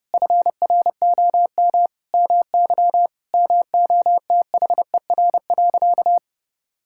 Morse code, heard as a keyed tone, about 30 wpm, 700 Hz. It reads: FROM MY MOTHER.
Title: Sentences from Top 1000 Words - Review Part 6 - 30wpm